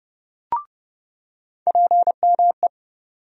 Morse code, heard as a keyed tone, about 32 wpm, 700 Hz. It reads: PME
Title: USA Call Sign Postfixes - 3 Letters 30wpm